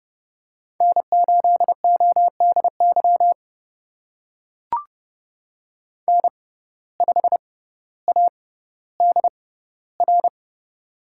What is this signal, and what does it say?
Morse code, heard as a keyed tone, about 32 wpm, spaced as for 16 wpm, 700 Hz. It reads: N8ODY N5ADR
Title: USA Call Signs - 5 Characters - 6x Character Spacing - 30wpm